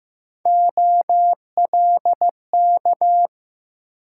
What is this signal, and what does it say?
Morse code, heard as a keyed tone, about 20 wpm, 700 Hz. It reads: OLK